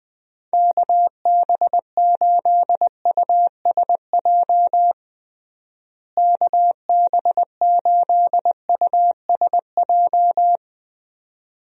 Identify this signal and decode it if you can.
Morse code, heard as a keyed tone, about 20 wpm, 700 Hz. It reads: KB8USJ KB8USJ
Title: USA Call Signs - 6 Characters - Repeated 3x - 20wpm